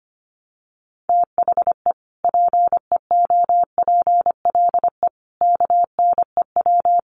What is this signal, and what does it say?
Morse code, heard as a keyed tone, about 25 wpm, 700 Hz. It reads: THE PEOPLE KNEW
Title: Sentences from Top 1100 Words 25wpm